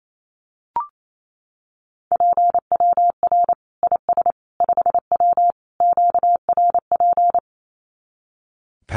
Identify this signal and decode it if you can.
Morse code, heard as a keyed tone, about 28 wpm, 700 Hz. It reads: PWR IS 5W QRP
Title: Common QSO Phrases 28wpm